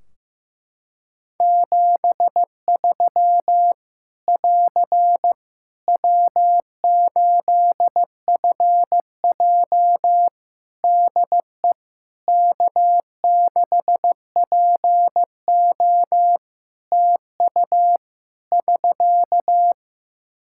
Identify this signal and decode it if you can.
Morse code, heard as a keyed tone, about 15 wpm, 700 Hz. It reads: 73 <AR> W8FJ DE K6PO TU <SK>
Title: Practice QSO - Exchange 4 - Ending - 15wpm